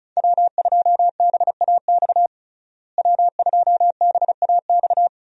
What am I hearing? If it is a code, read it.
Morse code, heard as a keyed tone, about 36 wpm, 700 Hz. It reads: W2BAX W2BAX
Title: USA Call Signs - 5 Characters - Repeated 2x - 35wpm